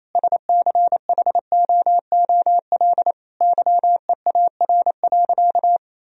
Morse code, heard as a keyed tone, about 28 wpm, 700 Hz. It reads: SCHOOL YEAR.
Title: Sentences from Top 1000 Words - Review Part 1 - 28wpm